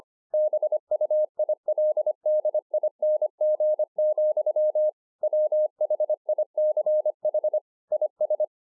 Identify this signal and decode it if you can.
Morse code, heard as a keyed tone, about 25 wpm, 610 Hz. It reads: BUILDING, WHICH IS